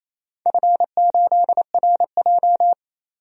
Morse code, heard as a keyed tone, about 28 wpm, 700 Hz. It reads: F8RJ
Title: DXCC-100 Call Signs - 4 Characters - 28wpm